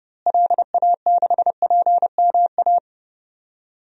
Morse code, heard as a keyed tone, about 30 wpm, 700 Hz. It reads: LA6PMA
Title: DXCC-100 Call Signs - 6 Characters - Repeated 2x - 30wpm